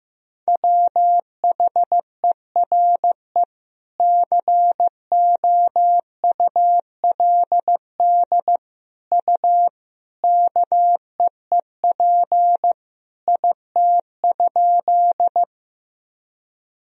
Morse code, heard as a keyed tone, about 15 wpm, 700 Hz. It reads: WHERE COULD U KEEP IT?
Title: Sentences from Top 300 Words and Abbreviations 15wpm